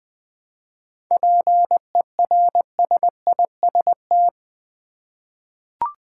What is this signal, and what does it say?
Morse code, heard as a keyed tone, about 20 wpm, 700 Hz. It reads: PERSIST
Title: 7 Letter Words - First 3 Letters Given - 20wpm